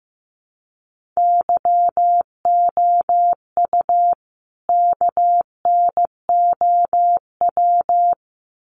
Morse code, heard as a keyed tone, about 15 wpm, 700 Hz. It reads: YOU KNOW
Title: Sets of 2 Words 15wpm